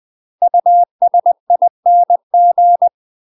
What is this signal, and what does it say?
Morse code, heard as a keyed tone, about 20 wpm, 700 Hz. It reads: USING
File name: UK MASTER QSOs TRAINING / 20WPM / UK MASTER QSOs 0071